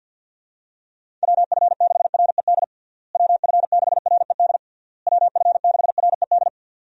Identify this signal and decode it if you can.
Morse code, heard as a keyed tone, about 52 wpm, 700 Hz. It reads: WF6RED WF6RED WF6RED
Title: USA Call Signs - 6 Characters - Repeated 3x - 50wpm